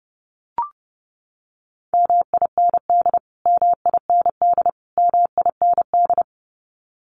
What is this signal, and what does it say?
Morse code, heard as a keyed tone, about 30 wpm, 700 Hz. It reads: MIND MIND MIND
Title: Top 500 Words Repeated 30wpm